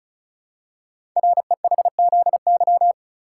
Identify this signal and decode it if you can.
Morse code, heard as a keyed tone, about 36 wpm, 700 Hz. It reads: REHZY
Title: Code Groups - 5 Letters - 35wpm